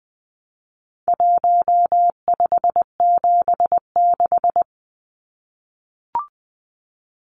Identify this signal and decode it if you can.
Morse code, heard as a keyed tone, about 20 wpm, 700 Hz. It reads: 1576
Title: Numbers - Contest Style - 4-Digit 20wpm